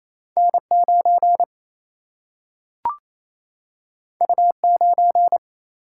Morse code, heard as a keyed tone, about 28 wpm, 700 Hz. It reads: N9 U9
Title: Single-Letter Single-Number 28wpm